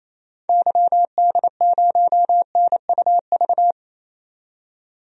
Morse code, heard as a keyed tone, about 28 wpm, 700 Hz. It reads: YD0NUV
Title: DXCC-100 Call Signs - 6 Characters - 28wpm